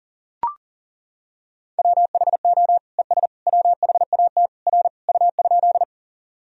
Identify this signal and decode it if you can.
Morse code, heard as a keyed tone, about 40 wpm, 700 Hz. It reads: WHO ES WHAT R U?
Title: Sentences from Top 100 Words and Abbreviations 40wpm